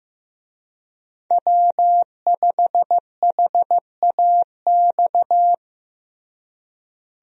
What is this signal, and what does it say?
Morse code, heard as a keyed tone, about 15 wpm, 700 Hz. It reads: W5HAX
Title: USA Call Signs - 5 Characters - Repeated 3x - 15wpm